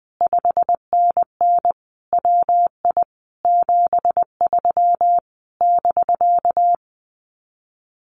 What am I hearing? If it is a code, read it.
Morse code, heard as a keyed tone, about 20 wpm, 700 Hz. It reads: 5NN WI 73 <BK>
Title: Common QSO Phrases Encore 20wpm